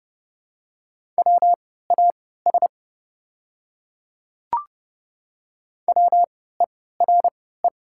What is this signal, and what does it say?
Morse code, heard as a keyed tone, about 30 wpm, spaced as for 17 wpm, 700 Hz. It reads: WAS WERE
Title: 3x Character Spacing - Top 100 Words - 30wpm